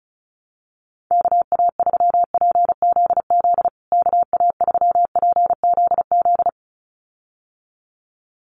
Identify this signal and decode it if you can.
Morse code, heard as a keyed tone, about 34 wpm, 700 Hz. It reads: KA3PZZ KA3PZZ
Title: USA Call Signs Repeated 2x 35wpm